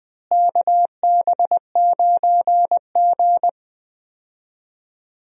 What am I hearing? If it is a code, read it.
Morse code, heard as a keyed tone, about 20 wpm, 700 Hz. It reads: KB9G
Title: USA Call Signs - 4 Characters - Repeated 2x - 20wpm